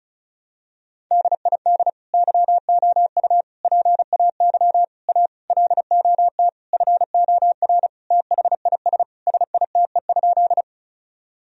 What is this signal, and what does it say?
Morse code, heard as a keyed tone, about 34 wpm, 700 Hz. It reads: DID YOU PAY A LOT FOR THIS SITE?
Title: Sentences from Top 1000 Words - Review Part 4 - 35wpm